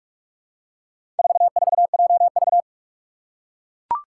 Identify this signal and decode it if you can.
Morse code, heard as a keyed tone, about 41 wpm, 700 Hz. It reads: 44JV
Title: Four Challenging Letter-Number Combinations 45wpm